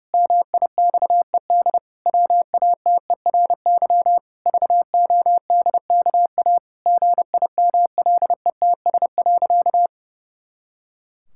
Morse code, heard as a keyed tone, about 30 wpm, 700 Hz. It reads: MIXED WATERY VODKA GIMLETS.
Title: Pangram Challenge v2 - 30wpm